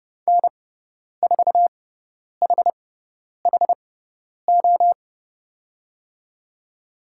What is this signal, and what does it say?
Morse code, heard as a keyed tone, about 30 wpm, spaced as for 10 wpm, 700 Hz. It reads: N4HHO
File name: Lesson 16g - Letter c - Call Signs 30wpm